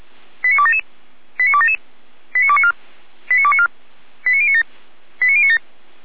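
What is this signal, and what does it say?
Five-tone selective call (ZVEI), heard as quick runs of five tones, tones of about 70 ms. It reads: 89277 89277 89294 89294 89097 89097
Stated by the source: ZVEI five-tone selective call received on an SDR.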